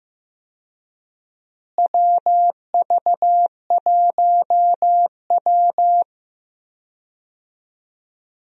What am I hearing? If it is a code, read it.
Morse code, heard as a keyed tone, about 15 wpm, 700 Hz. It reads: WV1W
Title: USA Call Signs - 4 Characters - Repeated 3x - 15wpm